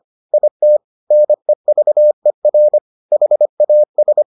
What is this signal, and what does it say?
Morse code, heard as a keyed tone, about 25 wpm, 600 Hz. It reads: IT NEVER HAS